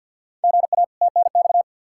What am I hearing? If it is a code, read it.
Morse code, heard as a keyed tone, about 51 wpm, 700 Hz. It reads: GA TNX
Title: K1USN SST QSO Practice 50wpm